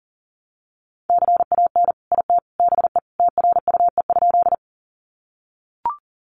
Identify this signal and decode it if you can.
Morse code, heard as a keyed tone, about 40 wpm, 700 Hz. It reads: CAN IT BE TRUE?
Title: Sentences from Top 400 Words 40wpm